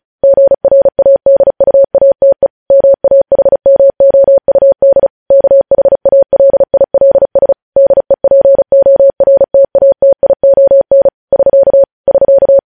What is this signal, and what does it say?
Morse code, heard as a keyed tone, about 36 wpm, 550 Hz. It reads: GRADUATE MAHMOUD KHALILS DEPORTATION <SK> <SK>
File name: podcast / News headlines-1x-35x35